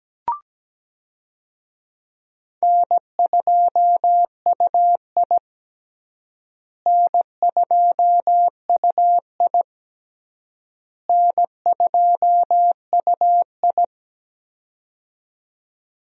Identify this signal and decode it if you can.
Morse code, heard as a keyed tone, about 17 wpm, 700 Hz. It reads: N2UI N2UI N2UI
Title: USA Call Signs - 4 Characters - Repeated 3x - 17wpm